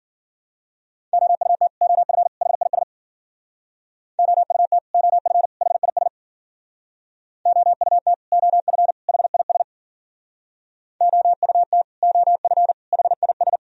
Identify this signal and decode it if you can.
Morse code, heard as a keyed tone, about 54 wpm, 700 Hz. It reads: OUT OF HIS OUT OF HIS OUT OF HIS OUT OF HIS